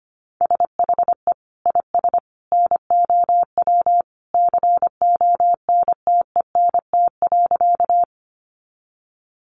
Morse code, heard as a keyed tone, about 25 wpm, 700 Hz. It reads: SHE IS NOW CONTENT.
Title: Sentences from Top 900 Words 25wpm